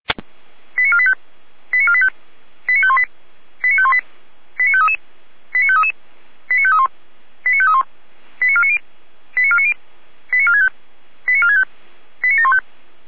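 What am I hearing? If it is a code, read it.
Five-tone selective call (ZVEI), heard as quick runs of five tones, tones of about 70 ms. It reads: 89485 89485 89518 89518 89533 89533 89531 89531 89490 89490 89465 89465 89715
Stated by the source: ZVEI five-tone selective call received on an SDR.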